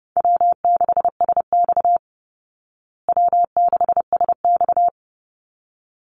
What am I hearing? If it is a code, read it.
Morse code, heard as a keyed tone, about 30 wpm, 700 Hz. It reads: W6SX W6SX
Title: Call Signs Repeated 3x - 30wpm